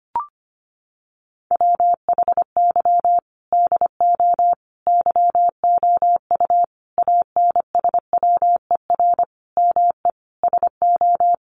Morse code, heard as a keyed tone, about 25 wpm, 700 Hz. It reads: WHY DO YOU ANSWER ME SO
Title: Sentences from Top 1500 Words 25wpm